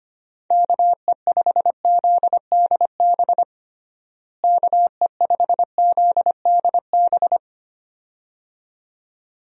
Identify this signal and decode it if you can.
Morse code, heard as a keyed tone, about 25 wpm, 700 Hz. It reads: KE5ZDB KE5ZDB
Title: USA Call Signs - 6 Characters - Repeated 2x - 25wpm